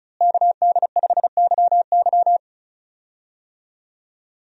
Morse code, heard as a keyed tone, about 34 wpm, 700 Hz. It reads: KD5YY
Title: USA Call Signs - 5 Characters - Repeated 2x - 35wpm